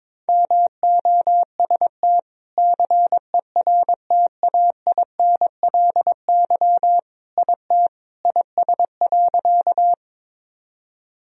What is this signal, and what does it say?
Morse code, heard as a keyed tone, about 22 wpm, 700 Hz. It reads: MOST CERTAINLY IT IS.
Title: Sentences from Top 1000 Words - Review Part 1 - 22wpm